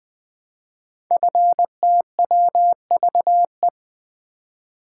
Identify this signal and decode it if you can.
Morse code, heard as a keyed tone, about 20 wpm, 700 Hz. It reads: FTWVE
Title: Code Groups - 5 Letters - 20wpm